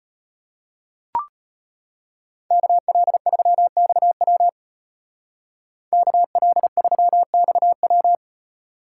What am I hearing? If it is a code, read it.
Morse code, heard as a keyed tone, about 34 wpm, 700 Hz. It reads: KL3XW KL3XW